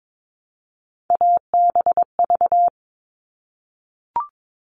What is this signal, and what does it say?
Morse code, heard as a keyed tone, about 22 wpm, 700 Hz. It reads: ABV
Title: All Cut Numbers - 3-Digit 22wpm